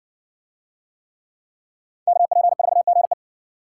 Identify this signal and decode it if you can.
Morse code, heard as a keyed tone, about 56 wpm, 700 Hz. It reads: KP4GE